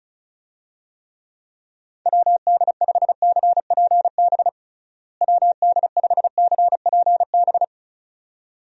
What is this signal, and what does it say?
Morse code, heard as a keyed tone, about 36 wpm, 700 Hz. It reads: WD5CPB WD5CPB
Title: USA Call Signs - 6 Characters - Repeated 2x - 35wpm